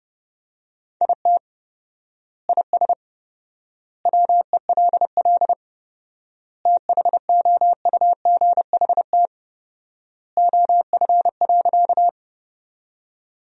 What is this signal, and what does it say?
Morse code, heard as a keyed tone, about 32 wpm, 700 Hz. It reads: IT IS WELL THOUGHT OF.
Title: 4x Word-Spacing - Sentences from Top 300 Words - 30wpm